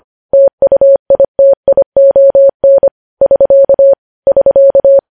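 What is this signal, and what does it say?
Morse code, heard as a keyed tone, about 25 wpm, 550 Hz. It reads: TUITION <SK> <SK>
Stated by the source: Morse code news headlines sent at 25 words per minute.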